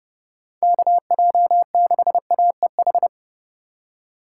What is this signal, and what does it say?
Morse code, heard as a keyed tone, about 30 wpm, 700 Hz. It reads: KJ6AEH